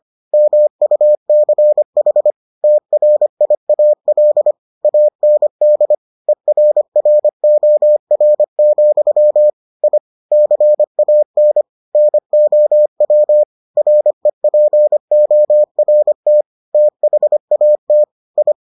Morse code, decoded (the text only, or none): MUCH TRIAL AND ERROR, I CAN NOW REPORT THAT I